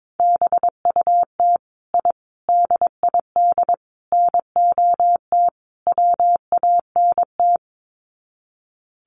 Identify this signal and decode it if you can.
Morse code, heard as a keyed tone, about 22 wpm, 700 Hz. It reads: BUT I DID NOT WANT
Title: Sets of 5 Words Encore 22wpm